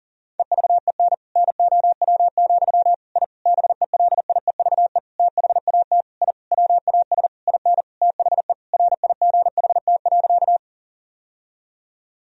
Morse code, heard as a keyed tone, about 40 wpm, 700 Hz. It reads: EVEN NOW, I BELIEVE THAT I WAS IN THE RIGHT.
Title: Sentences from Top 1000 Words - Review Part 1 - 40wpm